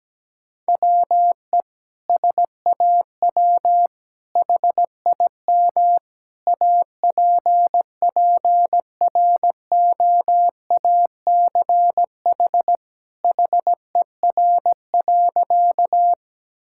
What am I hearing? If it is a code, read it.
Morse code, heard as a keyed tone, about 17 wpm, 700 Hz. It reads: WE SAW HIM APPROACH HER.
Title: Sentences from Top 1000 Words - Review Part 6 - 17wpm